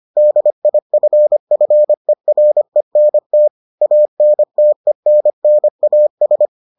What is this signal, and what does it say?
Morse code, heard as a keyed tone, about 25 wpm, 600 Hz. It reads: DIFFERENT ANTENNAS